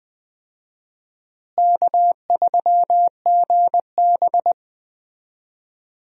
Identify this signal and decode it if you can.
Morse code, heard as a keyed tone, about 20 wpm, 700 Hz. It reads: K3GB
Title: USA Call Signs - 4 Characters - Repeated 3x - 20wpm